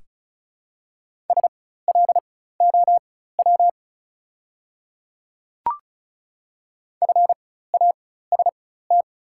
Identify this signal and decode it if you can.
Morse code, heard as a keyed tone, about 36 wpm, spaced as for 16 wpm, 700 Hz. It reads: SLOW FAST
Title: 4x Character Spacing - Top 500 Words - 35wpm